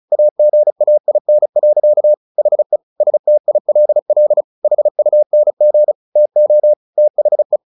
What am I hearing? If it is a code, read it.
Morse code, heard as a keyed tone, about 34 wpm, 600 Hz. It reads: AGAIN. HE STILL HUNG TO THE
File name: the old man and the sea 2 35wpm